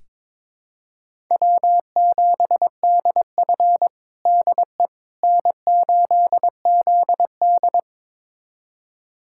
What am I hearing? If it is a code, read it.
Morse code, heard as a keyed tone, about 22 wpm, 700 Hz. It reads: W7DF DE N8ZD
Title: USA Callsign DE Callsign - 4 Characters - 22wpm